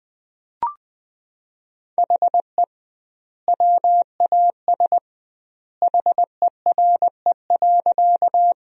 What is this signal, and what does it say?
Morse code, heard as a keyed tone, about 19 wpm, 700 Hz. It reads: HE WAS HERE.
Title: 2x Word-Spacing - Sentences from Top 200 Words - 20wpm